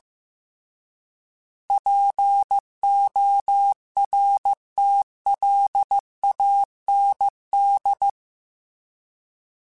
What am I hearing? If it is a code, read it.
Morse code, heard as a keyed tone, about 15 wpm, 800 Hz. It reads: PORTLAND
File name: fromSMs / FromSMs mp3s / Cities@13